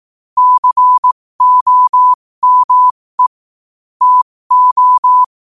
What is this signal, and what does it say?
Morse code, heard as a keyed tone, about 18 wpm, spaced as for 15 wpm, 1000 Hz. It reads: COME TO
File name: qotdfiles / 15wpm